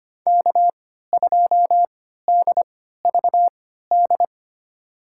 Morse code, heard as a keyed tone, about 25 wpm, spaced as for 14 wpm, 700 Hz. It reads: K2DVD
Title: USA Call Signs - 5 Characters - 3x Character Spacing - 25wpm